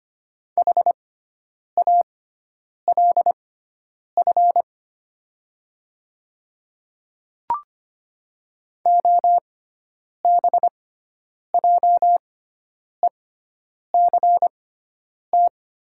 Morse code, heard as a keyed tone, about 25 wpm, spaced as for 9 wpm, 700 Hz. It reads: HALF OBJECT
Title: 6x Character Spacing - Top 500 Words - 25wpm